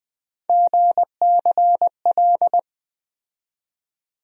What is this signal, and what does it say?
Morse code, heard as a keyed tone, about 20 wpm, 700 Hz. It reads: GCL
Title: USA Call Sign Postfixes - 3 Letters 20wpm